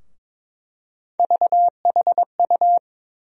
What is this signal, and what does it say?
Morse code, heard as a keyed tone, about 22 wpm, 700 Hz. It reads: VHU
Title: USA Call Sign Postfixes - 3 Letters 22wpm